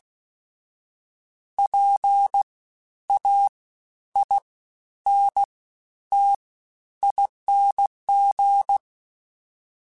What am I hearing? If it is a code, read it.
Morse code, heard as a keyed tone, about 16 wpm, spaced as for 10 wpm, 800 Hz. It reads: PAINTING
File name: Special INGs-mp3s / Special INGs-mp3s / Fast ing @15